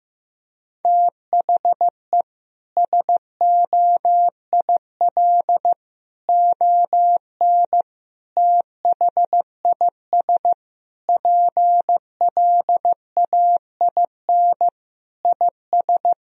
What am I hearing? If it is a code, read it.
Morse code, heard as a keyed tone, about 15 wpm, 700 Hz. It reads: THE SOIL ON THIS PLAIN IS
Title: Sentences from Top 1000 Words - Review Part 4 - 15wpm